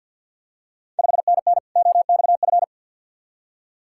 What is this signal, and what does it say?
Morse code, heard as a keyed tone, about 48 wpm, 690 Hz. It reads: 5NN OXF
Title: Ontario QSO Party Exchange 50wpm